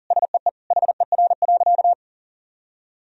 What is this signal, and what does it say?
Morse code, heard as a keyed tone, about 40 wpm, 700 Hz. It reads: SEE HER.
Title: Sentences from Top 1000 Words - Review Part 4 - 40wpm